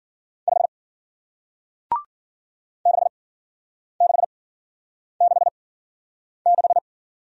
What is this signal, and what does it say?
Morse code, heard as a keyed tone, about 49 wpm, spaced as for 9 wpm, 700 Hz. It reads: 5 6666
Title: Speed-Racing Single Letter-Number 40wpm